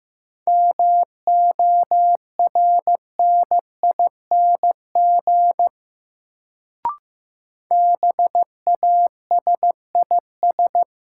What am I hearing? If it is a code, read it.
Morse code, heard as a keyed tone, about 15 wpm, 700 Hz. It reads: MORNING BASIS
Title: Sets of 3 Words - New Word Drill - 15wpm